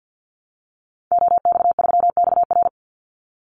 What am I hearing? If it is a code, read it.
Morse code, heard as a keyed tone, about 52 wpm, 700 Hz. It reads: OX3XR
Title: Call Signs Repeated 3x - 50wpm